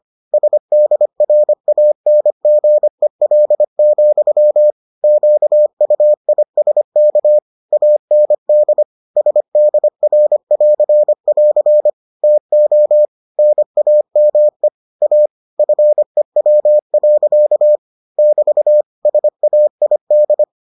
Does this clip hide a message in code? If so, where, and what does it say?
Yes: Morse code from 0.3 to 20.4 s, heard as a keyed tone, about 25 wpm, 600 Hz. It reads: SDRANGEL, QUISK AND SDR<AR><AR> TO NAME A FEW. <BT> SAID